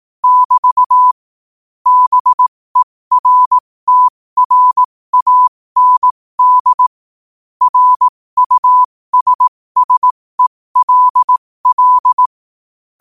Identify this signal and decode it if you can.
Morse code, heard as a keyed tone, about 18 wpm, spaced as for 15 wpm, 1000 Hz. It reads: <BT> BERTRAND RUSSELL
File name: qotdfiles / 15wpm